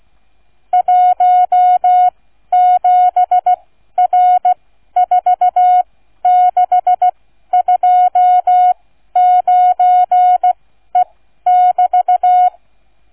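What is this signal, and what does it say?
Morse code, heard as a keyed tone, about 16 wpm, spaced as for 12 wpm, 730 Hz. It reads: 17R4629E<BT>
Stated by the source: Morse code receive practice, flight number, from ZL3TK.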